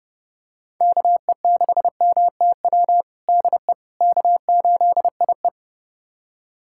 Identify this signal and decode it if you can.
Morse code, heard as a keyed tone, about 32 wpm, 700 Hz. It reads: KE6MTW DE K8IE
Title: USA Callsign DE Callsign 30wpm